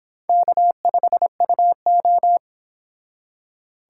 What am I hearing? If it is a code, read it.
Morse code, heard as a keyed tone, about 25 wpm, 700 Hz. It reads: K5UO